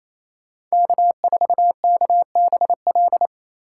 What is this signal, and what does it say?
Morse code, heard as a keyed tone, about 28 wpm, 700 Hz. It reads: K4KBL
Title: K1USN SST QSO Practice - 28wpm